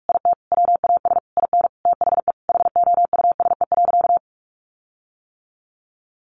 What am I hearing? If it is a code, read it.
Morse code, heard as a keyed tone, about 44 wpm, 700 Hz. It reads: IT WAS IN THE HOUSE.
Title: Sentences from Top 1500 Words 45wpm